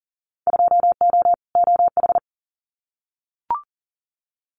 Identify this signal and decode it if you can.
Morse code, heard as a keyed tone, about 38 wpm, 700 Hz. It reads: 2O OH
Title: Winter Field Day Exchanges 40wpm